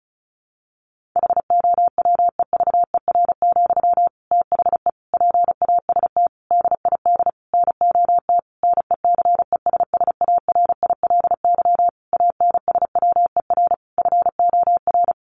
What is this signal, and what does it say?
Morse code, heard as a keyed tone, about 34 wpm, 700 Hz. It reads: HOWEVER, THE PAST DID NOT NECESSARILY ANSWER FOR
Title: Sentences from Top 1400 Words 35wpm